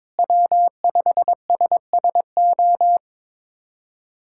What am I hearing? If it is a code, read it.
Morse code, heard as a keyed tone, about 22 wpm, 700 Hz. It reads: W5SSO
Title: USA Callsign DE Callsign - 5 Characters - 22wpm